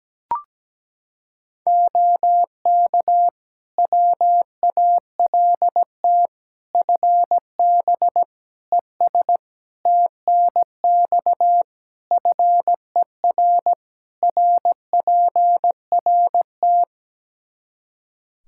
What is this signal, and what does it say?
Morse code, heard as a keyed tone, about 17 wpm, 700 Hz. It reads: OK WALT FB ES TNX FER RPRT
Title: Practice QSO - Exchange 2 - Info - 17wpm